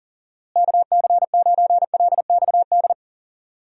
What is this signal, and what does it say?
Morse code, heard as a keyed tone, about 40 wpm, 700 Hz. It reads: KC9LXD
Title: USA Call Signs - 6 Characters - Repeated 3x - 40wpm